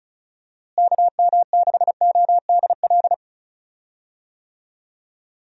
Morse code, heard as a keyed tone, about 34 wpm, 700 Hz. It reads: KM6ODL
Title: USA Call Signs - 6 Characters - Repeated 2x - 35wpm